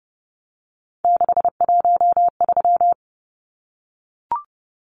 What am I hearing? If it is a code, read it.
Morse code, heard as a keyed tone, about 30 wpm, 700 Hz. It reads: 613
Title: Numbers - 3-Digit 30wpm